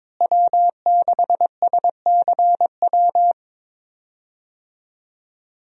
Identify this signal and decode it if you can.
Morse code, heard as a keyed tone, about 22 wpm, 700 Hz. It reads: W6SCW
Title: USA Call Signs - 5 Characters - Repeated 2x - 22wpm